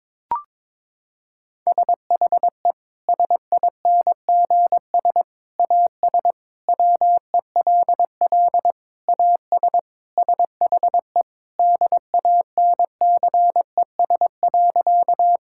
Morse code, heard as a keyed tone, about 22 wpm, 700 Hz. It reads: SHE SINGS AS WELL AS SHE DANCES.
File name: practice / 22wpm / Sentences with 3-Word Phrases - 22wpm